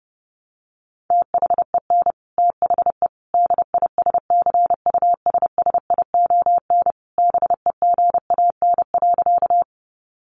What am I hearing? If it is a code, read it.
Morse code, heard as a keyed tone, about 30 wpm, 700 Hz. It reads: THEN THE DISCUSSION BEGAN.
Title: Sentences from Top 1100 Words 30wpm